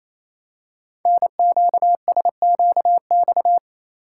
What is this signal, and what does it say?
Morse code, heard as a keyed tone, about 28 wpm, 700 Hz. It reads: NQSQX
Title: Code Groups - 5 Letters - 28wpm